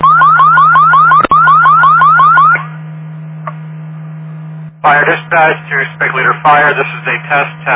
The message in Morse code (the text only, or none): none